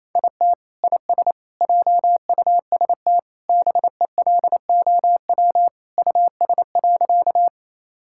Morse code, heard as a keyed tone, about 28 wpm, 700 Hz. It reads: IT IS JUST BELOW US.
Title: Sentences from Top 1000 Words - Review Part 5 - 28wpm